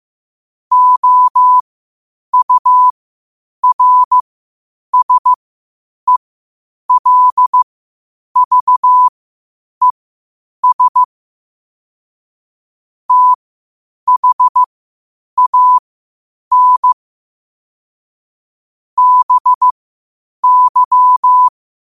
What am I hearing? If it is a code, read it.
Morse code, heard as a keyed tone, about 15 wpm, spaced as for 8 wpm, 1000 Hz. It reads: OURSELVES THAN BY
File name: qotdfiles / 8wpm